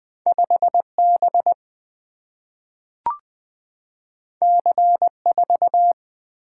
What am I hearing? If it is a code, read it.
Morse code, heard as a keyed tone, about 20 wpm, 700 Hz. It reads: 5 B C 4
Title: Single-Letter Single-Number 20wpm